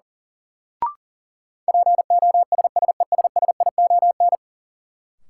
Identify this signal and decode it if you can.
Morse code, heard as a keyed tone, about 37 wpm, 700 Hz. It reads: POSSESSION
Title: Sentences from Top 1400 Words - New Word Drill Alt - 40wpm